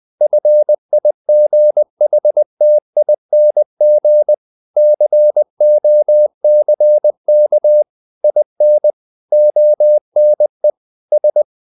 Morse code, heard as a keyed tone, about 20 wpm, 600 Hz. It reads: FIGHTING COCK IN ONE S
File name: the old man and the sea / the old man and the sea part 1 20wpm